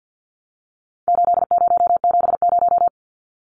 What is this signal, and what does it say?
Morse code, heard as a keyed tone, about 50 wpm, 700 Hz. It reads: 8070
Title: Numbers - 4-Digit 50wpm